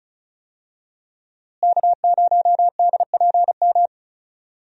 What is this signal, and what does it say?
Morse code, heard as a keyed tone, about 34 wpm, 700 Hz. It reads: K0DPM
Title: USA Call Signs - 5 Characters - Repeated 2x - 35wpm